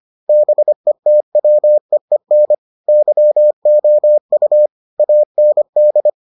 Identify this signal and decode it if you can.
Morse code, heard as a keyed tone, about 25 wpm, 600 Hz. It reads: BETWEEN YOU AND